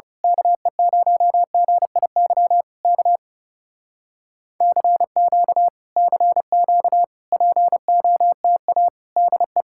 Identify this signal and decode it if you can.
Morse code, heard as a keyed tone, about 28 wpm, 700 Hz. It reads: KE0GIY K CQ CQ POTA DE
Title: Speed-Racing Common QSO Phrases Encore - 30wpm